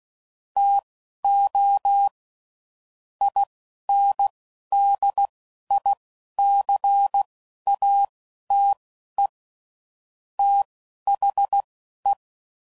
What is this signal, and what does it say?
Morse code, heard as a keyed tone, about 16 wpm, spaced as for 11 wpm, 800 Hz. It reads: TO INDICATE THE
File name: CW Code Groups @10WPM / Technical Subjects / Gates Intro